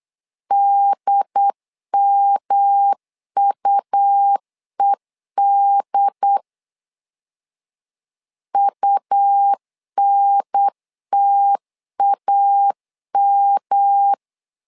Morse code, heard as a keyed tone, about 8 wpm, 800 Hz. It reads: DMUED UNTAM
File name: LEZIONI / 3 MWRO / 3 M 8wpm 2300